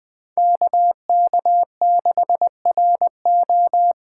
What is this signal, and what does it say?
Morse code, heard as a keyed tone, about 20 wpm, 700 Hz. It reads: KK6RO
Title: USA Call Signs - 5 Characters - 3x Character Spacing - 20wpm